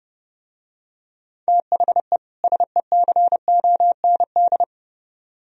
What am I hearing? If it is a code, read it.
Morse code, heard as a keyed tone, about 31 wpm, 700 Hz. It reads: THE SECOND